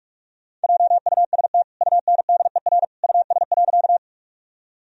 Morse code, heard as a keyed tone, about 46 wpm, 700 Hz. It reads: JUST UNDER US.
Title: Sentences from Top 1000 Words - Review Part 1 - 45wpm